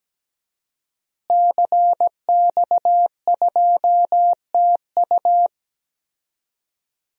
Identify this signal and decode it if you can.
Morse code, heard as a keyed tone, about 17 wpm, 700 Hz. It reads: CX2TU
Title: DXCC-100 Call Signs - 5 Characters - Repeated 2x - 17wpm